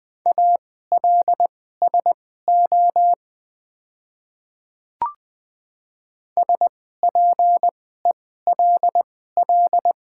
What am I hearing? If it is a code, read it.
Morse code, heard as a keyed tone, about 20 wpm, spaced as for 14 wpm, 700 Hz. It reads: ALSO SPELL
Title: 2x Character Spacing - Top 300 Words - 20wpm